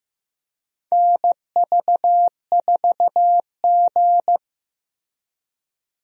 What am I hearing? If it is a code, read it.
Morse code, heard as a keyed tone, about 15 wpm, 700 Hz. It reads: NV4G
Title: USA Call Signs - 4 Characters - Repeated 3x - 15wpm